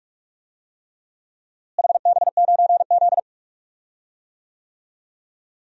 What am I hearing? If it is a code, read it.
Morse code, heard as a keyed tone, about 46 wpm, 700 Hz. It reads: HB9Z
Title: DXCC-100 Call Signs - 4 Characters - Repeated 3x - 45wpm